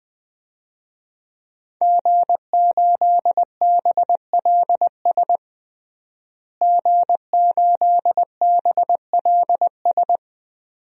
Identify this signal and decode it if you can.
Morse code, heard as a keyed tone, about 20 wpm, 700 Hz. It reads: G8BLS G8BLS
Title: DXCC-100 Call Signs - 5 Characters - Repeated 2x - 20wpm